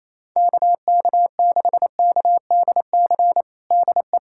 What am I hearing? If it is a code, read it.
Morse code, heard as a keyed tone, about 28 wpm, 700 Hz. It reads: KK6KDC DE
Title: USA Callsign DE Callsign 28wpm